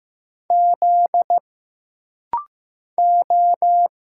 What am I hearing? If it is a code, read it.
Morse code, heard as a keyed tone, about 15 wpm, 700 Hz. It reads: Z O
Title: Single Letter-Number Alt - Rapid-Fire 15wpm